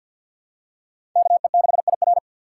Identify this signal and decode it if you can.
Morse code, heard as a keyed tone, about 48 wpm, 700 Hz. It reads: KE6IR
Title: USA Call Signs - 5 Characters - 2x Character Spacing - 50wpm